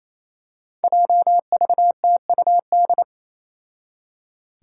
Morse code, heard as a keyed tone, about 28 wpm, 700 Hz. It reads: JVTUD
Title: Five Challenging Letter-Number Combinations 28wpm